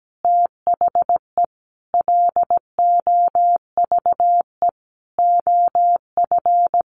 Morse code, heard as a keyed tone, about 17 wpm, 700 Hz. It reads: THE LOVE OF
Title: Sentences from Top 1400 Words 17wpm